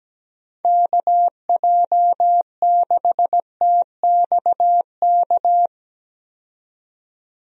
Morse code, heard as a keyed tone, about 17 wpm, 700 Hz. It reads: KJ6TXK
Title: USA Call Signs - 6 Characters - 17wpm